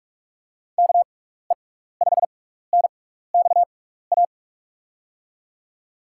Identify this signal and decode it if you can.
Morse code, heard as a keyed tone, about 44 wpm, spaced as for 15 wpm, 700 Hz. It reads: KE5NXA
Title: USA Call Signs - 6 Characters - 6x Character Spacing - 45wpm